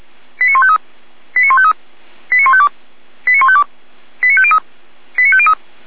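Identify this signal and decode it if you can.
Five-tone selective call (ZVEI), heard as quick runs of five tones, tones of about 70 ms. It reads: 89153 89153 89152 89152 89503 89503
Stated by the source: ZVEI five-tone selective call 89165 received by SDR.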